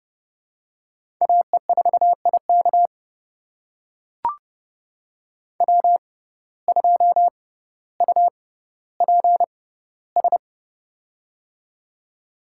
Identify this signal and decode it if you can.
Morse code, heard as a keyed tone, about 30 wpm, spaced as for 15 wpm, 700 Hz. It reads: AE4IK W2UPS
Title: USA Call Signs - 6x Character Spacing - 30wpm